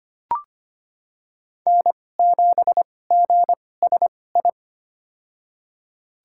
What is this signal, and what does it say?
Morse code, heard as a keyed tone, about 25 wpm, spaced as for 18 wpm, 700 Hz. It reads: N7GSI
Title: USA Call Signs - 2x Character Spacing - 25wpm